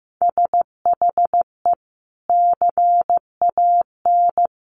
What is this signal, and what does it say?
Morse code, heard as a keyed tone, about 15 wpm, 700 Hz. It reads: SHE CAN
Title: Sentences from Top 1000 Words 15wpm